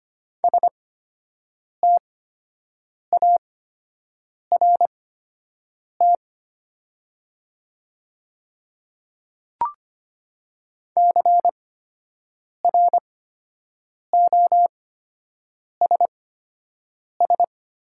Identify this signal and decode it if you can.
Morse code, heard as a keyed tone, about 25 wpm, spaced as for 7 wpm, 700 Hz. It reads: START CROSS
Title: 8x Character Spacing - Top 500 Words - 25wpm